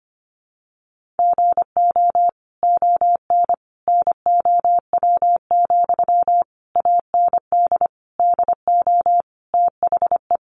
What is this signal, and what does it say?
Morse code, heard as a keyed tone, about 25 wpm, 700 Hz. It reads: GO ON NOW, AND DO THE
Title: Sentences from Top 1500 Words 25wpm